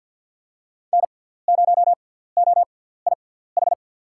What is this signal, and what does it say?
Morse code, heard as a keyed tone, about 47 wpm, spaced as for 17 wpm, 700 Hz. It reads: N0OIH
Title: USA Call Signs - 5 Characters - 6x Character Spacing - 50wpm